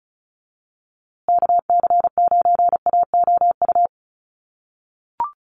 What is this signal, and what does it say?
Morse code, heard as a keyed tone, about 34 wpm, 700 Hz. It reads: KC9AOU
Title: USA Call Signs Repeated 3x 35wpm